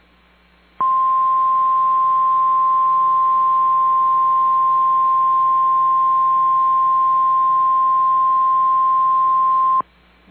Emergency Alert System (SAME) tones (the 1050 Hz attention tone). It is heard from 0.7 to 9.8 s.
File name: EAS Logs / KMNR